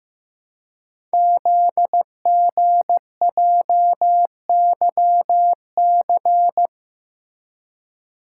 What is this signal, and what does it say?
Morse code, heard as a keyed tone, about 15 wpm, 700 Hz. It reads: ZGJYC
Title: Code Groups - 5 Letter:Numbers - 15wpm